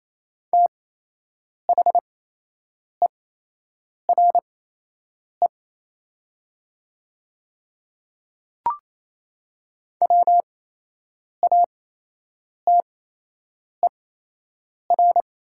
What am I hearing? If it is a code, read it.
Morse code, heard as a keyed tone, about 28 wpm, spaced as for 8 wpm, 700 Hz. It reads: THERE WATER
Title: 8x Character Spacing - Top 300 Words - 28wpm